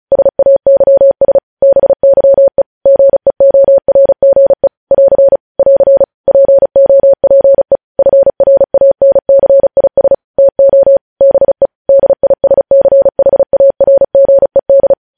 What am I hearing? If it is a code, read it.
Morse code, heard as a keyed tone, about 36 wpm, 550 Hz. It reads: SAYS BYE GEORGE <AR> <AR> POPE FRANCIS TO BE DISCHARGED
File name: podcast / News headlines-1x-35x35